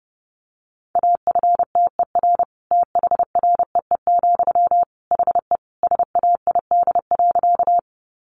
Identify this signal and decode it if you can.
Morse code, heard as a keyed tone, about 30 wpm, 700 Hz. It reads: AFTER THREE, HE SAID.
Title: Sentences from Top 200 Words 30wpm